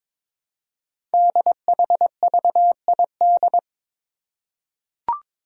Morse code, heard as a keyed tone, about 22 wpm, 700 Hz. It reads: DHVID
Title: Code Groups - 5 Letters - 22wpm